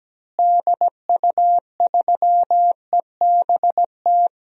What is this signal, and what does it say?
Morse code, heard as a keyed tone, about 17 wpm, 700 Hz. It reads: DU3EBT
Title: DXCC-100 Call Signs - 6 Characters - Repeated 2x - 17wpm